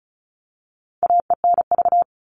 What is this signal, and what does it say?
Morse code, heard as a keyed tone, about 34 wpm, 700 Hz. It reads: AENV
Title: All Cut Numbers - 4-Digit 35wpm